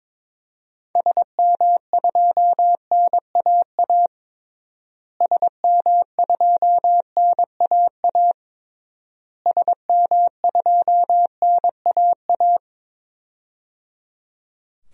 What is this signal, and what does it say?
Morse code, heard as a keyed tone, about 22 wpm, 700 Hz. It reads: SM2NAA SM2NAA SM2NAA
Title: DXCC-100 Call Signs - 6 Characters - Repeated 3x - 22wpm